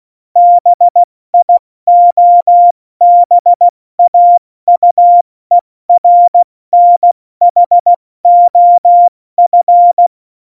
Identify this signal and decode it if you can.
Morse code, heard as a keyed tone, about 16 wpm, spaced as for 14 wpm, 700 Hz. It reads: BIOBAUERNHOF
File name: Morsekurs / Witze / Witz 14 Juli 2020, 70 F80